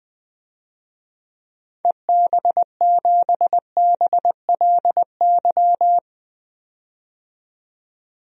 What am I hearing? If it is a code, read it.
Morse code, heard as a keyed tone, about 20 wpm, 700 Hz. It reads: EB7BLY
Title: DXCC-100 Call Signs - 6 Characters - 20wpm